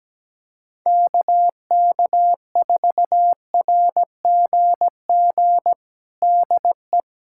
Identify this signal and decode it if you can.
Morse code, heard as a keyed tone, about 17 wpm, 700 Hz. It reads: KK4RGG DE
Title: USA Callsign DE Callsign 17wpm